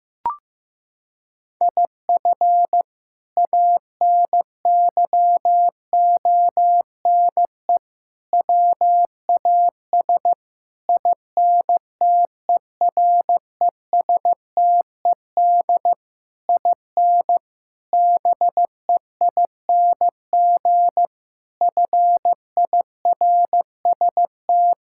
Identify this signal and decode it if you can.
Morse code, heard as a keyed tone, about 15 wpm, 700 Hz. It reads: IF ANYONE WAS INTERESTED IN BEING FIRST
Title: Sentences from Top 1600 Words 15wpm